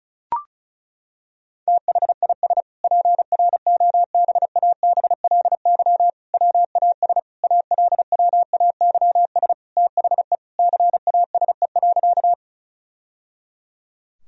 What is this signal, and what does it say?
Morse code, heard as a keyed tone, about 36 wpm, 700 Hz. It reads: THIS PROBABLY WAS ALWAYS THE CASE.
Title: Sentences from Top 1000 Words - Review Part 4 - 35wpm